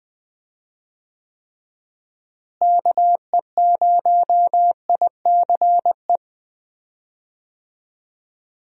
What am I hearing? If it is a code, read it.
Morse code, heard as a keyed tone, about 21 wpm, 700 Hz. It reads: KE0ICE
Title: USA Call Signs - 6 Characters - Repeated 3x - 20wpm